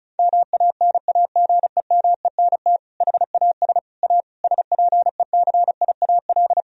Morse code, heard as a keyed tone, about 34 wpm, 700 Hz. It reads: MANAGEMENT HAS A SPECIAL
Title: Sentences from Top 1000 Words - Review Part 4 - 35wpm